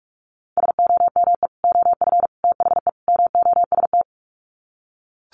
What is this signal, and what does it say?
Morse code, heard as a keyed tone, about 44 wpm, 700 Hz. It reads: SOME OF THE MOST
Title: Sets of 4 Words 45wpm